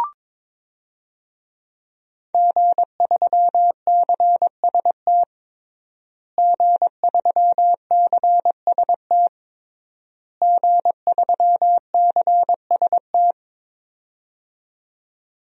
Morse code, heard as a keyed tone, about 22 wpm, 700 Hz. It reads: G3CST G3CST G3CST
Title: DXCC-100 Call Signs - 5 Characters - Repeated 3x - 22wpm